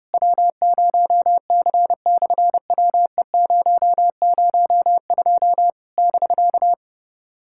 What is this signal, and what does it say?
Morse code, heard as a keyed tone, about 30 wpm, 700 Hz. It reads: W0C/WE002 <BK>
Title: SOTA QSO Practice 30wpm